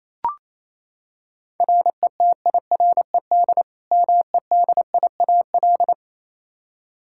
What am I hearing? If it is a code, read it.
Morse code, heard as a keyed tone, about 27 wpm, 700 Hz. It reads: RETIRED MEDIAL
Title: Practice QSO - Exchange 3 - More Info - 28wpm